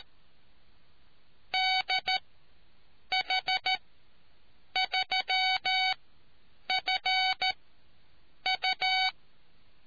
Morse code, heard as a keyed tone, about 13 wpm, spaced as for 7 wpm, 750 Hz. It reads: DH3FU